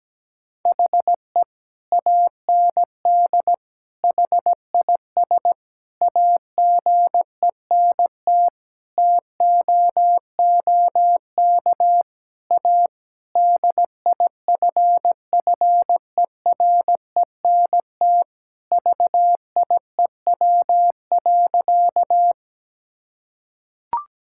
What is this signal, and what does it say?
Morse code, heard as a keyed tone, about 17 wpm, 700 Hz. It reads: HE AND HIS AGENT TOOK A DIFFERENT VIEW.
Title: Sentences from Top 1600 Words 17wpm